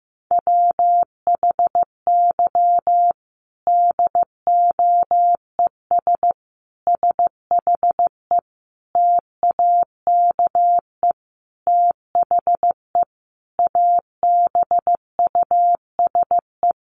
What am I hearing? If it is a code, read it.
Morse code, heard as a keyed tone, about 15 wpm, 700 Hz. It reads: WHY DOES SHE TAKE THE ABUSE
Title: Sentences from Top 1300 Words 15wpm